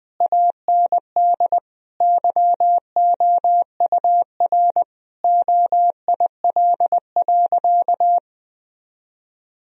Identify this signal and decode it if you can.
Morse code, heard as a keyed tone, about 20 wpm, 700 Hz. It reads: AND YOUR OIL.
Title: Sentences from Top 1000 Words - Review Part 4 - 20wpm